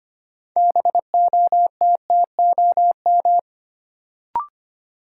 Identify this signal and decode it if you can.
Morse code, heard as a keyed tone, about 25 wpm, 700 Hz. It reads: BOTTOM
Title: Sentences from Top 900 Words - New Word Drill - 25wpm